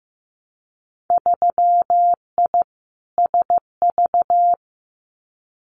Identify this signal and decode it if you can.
Morse code, heard as a keyed tone, about 15 wpm, 700 Hz. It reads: 3I SV
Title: Winter Field Day Exchanges Encore 15wpm